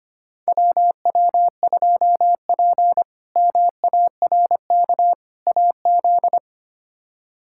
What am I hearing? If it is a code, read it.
Morse code, heard as a keyed tone, about 25 wpm, 700 Hz. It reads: WW2P MARK AZ
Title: K1USN SST QSO Practice - 25wpm